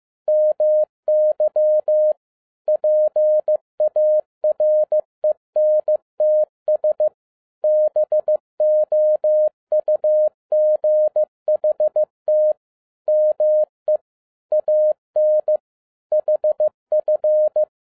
Morse code, heard as a keyed tone, about 15 wpm, 610 Hz. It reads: MY PARENTS BOUGHT ME AN HF